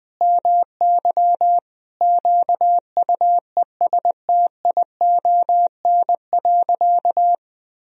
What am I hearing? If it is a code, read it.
Morse code, heard as a keyed tone, about 21 wpm, 700 Hz. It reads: MY QUESTION.
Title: Sentences from Top 1000 Words - Review Part 2 - 20wpm